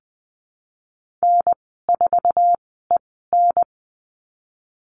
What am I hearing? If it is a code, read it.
Morse code, heard as a keyed tone, about 20 wpm, spaced as for 15 wpm, 700 Hz. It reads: N4EN
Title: 048 Lesson 9g - Number 4 - Call Signs 20wpm